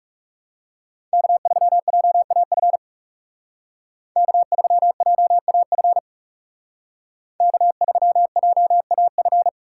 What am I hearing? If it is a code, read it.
Morse code, heard as a keyed tone, about 39 wpm, 700 Hz. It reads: K3JAF K3JAF K3JAF